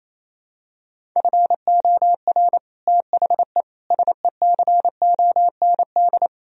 Morse code, heard as a keyed tone, about 28 wpm, 700 Hz. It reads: FOR THE SECOND
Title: Sentences from Top 1000 Words - Review Part 1 - 28wpm